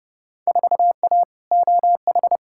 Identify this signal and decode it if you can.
Morse code, heard as a keyed tone, about 32 wpm, 700 Hz. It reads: 4A OH
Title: Field Day Exchanges - 30wpm